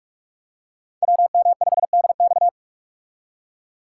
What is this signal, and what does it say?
Morse code, heard as a keyed tone, about 47 wpm, 700 Hz. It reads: WM5DX
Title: K1USN SST QSO Practice - 45wpm